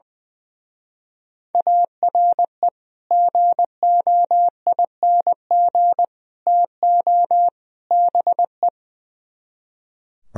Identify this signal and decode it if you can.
Morse code, heard as a keyed tone, about 20 wpm, 700 Hz. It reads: ARE GOING TO BE
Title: Speed-Racing - Sets of 4 Words 20wpm